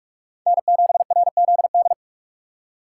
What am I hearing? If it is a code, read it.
Morse code, heard as a keyed tone, about 46 wpm, 700 Hz. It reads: N7RZD
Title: USA Call Signs - 5 Characters - 2x Character Spacing - 45wpm